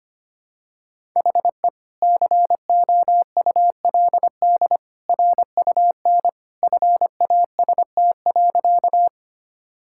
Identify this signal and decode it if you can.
Morse code, heard as a keyed tone, about 25 wpm, 700 Hz. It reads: HE COULD RUN FAST.
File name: practice / 25wpm / Sentences with 2-Word Phrases - 25wpm